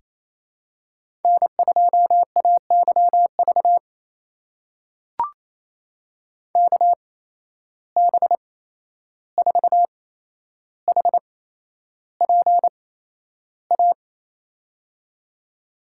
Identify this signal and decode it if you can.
Morse code, heard as a keyed tone, about 28 wpm, spaced as for 11 wpm, 700 Hz. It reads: N2AYV KB4HPA
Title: USA Call Signs - 8x Character Spacing - 28wpm